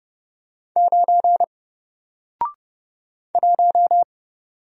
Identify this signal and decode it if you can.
Morse code, heard as a keyed tone, about 30 wpm, 700 Hz. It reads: 9 1
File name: Single-Letter-Number-30wpm